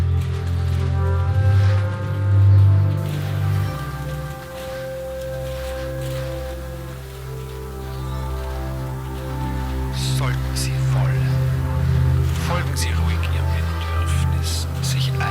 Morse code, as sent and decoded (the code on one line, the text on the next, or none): none